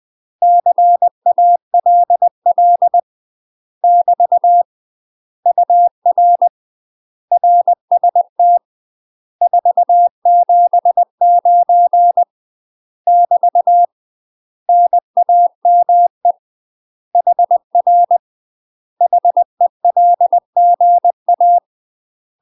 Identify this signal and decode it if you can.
Morse code, heard as a keyed tone, about 20 wpm, 700 Hz. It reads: CALL <BT> UR RST 479 <BT> NAME HR HELGA
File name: UK MASTER QSOs TRAINING / 20WPM / UK MASTER QSOs 0008